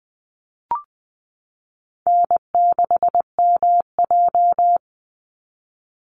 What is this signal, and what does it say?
Morse code, heard as a keyed tone, about 20 wpm, 700 Hz. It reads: N6MJ
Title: Easy Call Signs - 20wpm